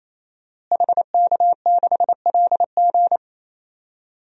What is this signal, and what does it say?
Morse code, heard as a keyed tone, about 28 wpm, 700 Hz. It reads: HK6LG
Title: DXCC-100 Call Signs - 5 Characters - Repeated 3x - 28wpm